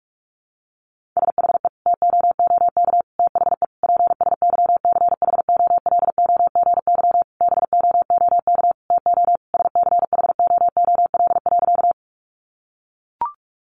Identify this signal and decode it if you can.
Morse code, heard as a keyed tone, about 44 wpm, 700 Hz. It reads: SHE TOOK THE PSYCHOLOGY BOOK TO SCHOOL.
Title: Sentences from Top 1400 Words 45wpm